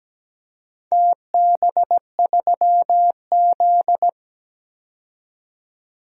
Morse code, heard as a keyed tone, about 17 wpm, 700 Hz. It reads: TB3Z
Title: DXCC-100 Call Signs - 4 Characters - 17wpm